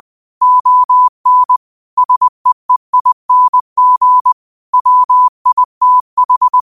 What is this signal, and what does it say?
Morse code, heard as a keyed tone, about 20 wpm, 1000 Hz. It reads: ON SEEING WITH
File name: qotdfiles / 20wpm